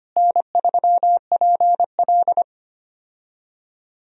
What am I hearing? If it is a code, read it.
Morse code, heard as a keyed tone, about 25 wpm, 700 Hz. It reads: N3PL
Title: USA Call Signs - 4 Characters - Repeated 3x - 25wpm